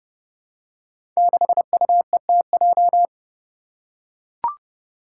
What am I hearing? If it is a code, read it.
Morse code, heard as a keyed tone, about 30 wpm, 700 Hz. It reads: 6UETJ
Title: Five Challenging Letter-Number Combinations 30wpm